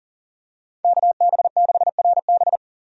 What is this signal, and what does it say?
Morse code, heard as a keyed tone, about 37 wpm, 700 Hz. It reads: KB6RB
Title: USA Call Signs - 5 Characters - 40wpm